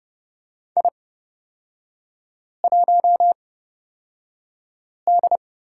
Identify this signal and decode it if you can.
Morse code, heard as a keyed tone, about 29 wpm, spaced as for 5 wpm, 690 Hz. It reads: I1D
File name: course / 30:10wpm / 236 Lesson 38e - Letter q - Three Characters 30wpm